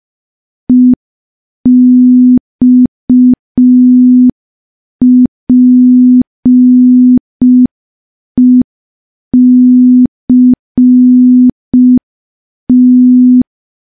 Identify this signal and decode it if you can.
Morse code, heard as a keyed tone, about 5 wpm, 250 Hz. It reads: EXPECT
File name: morse / morse-training-05WPM